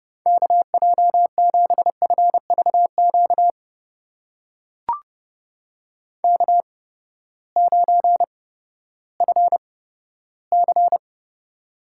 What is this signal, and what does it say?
Morse code, heard as a keyed tone, about 30 wpm, spaced as for 15 wpm, 700 Hz. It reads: KJ7FVQ K9FC
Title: USA Call Signs - 8x Character Spacing - 30wpm